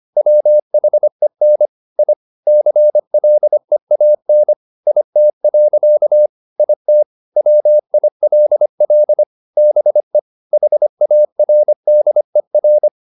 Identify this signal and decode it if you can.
Morse code, heard as a keyed tone, about 25 wpm, 600 Hz. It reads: WHEN I CLEAN IT. IT WILL BE HARDER